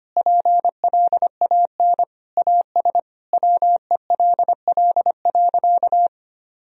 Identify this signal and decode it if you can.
Morse code, heard as a keyed tone, about 25 wpm, 700 Hz. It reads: PLAN AS WELL.
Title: Sentences from Top 1000 Words - Review Part 2 - 25wpm